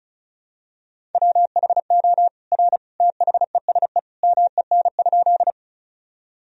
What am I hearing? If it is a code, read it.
Morse code, heard as a keyed tone, about 36 wpm, 700 Hz. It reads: WHO R THESE MEN?
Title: Sentences from Top 200 Words and Abbreviations 35wpm